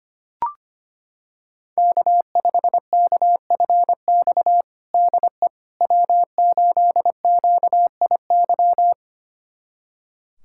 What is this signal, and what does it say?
Morse code, heard as a keyed tone, about 25 wpm, 700 Hz. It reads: K5KFX DE W8QIY
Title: USA Callsign DE Callsign - 5 Characters - 25wpm